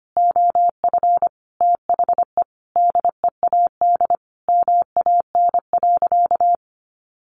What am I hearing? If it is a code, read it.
Morse code, heard as a keyed tone, about 25 wpm, 700 Hz. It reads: OF THE DEAD MAN.
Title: Sentences from Top 1500 Words 25wpm